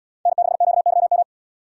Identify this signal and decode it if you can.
Morse code, heard as a keyed tone, about 72 wpm, 700 Hz. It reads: N6QYM